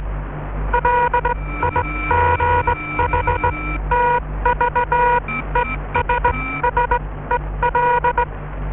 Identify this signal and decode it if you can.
Morse code, heard as a keyed tone, about 20 wpm, spaced as for 15 wpm, 960 Hz. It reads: LIGHTVESSEL